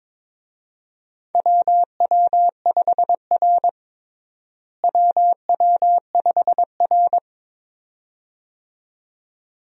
Morse code, heard as a keyed tone, about 22 wpm, 700 Hz. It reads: WW5R WW5R
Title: USA Call Signs - 4 Characters - Repeated 2x - 22wpm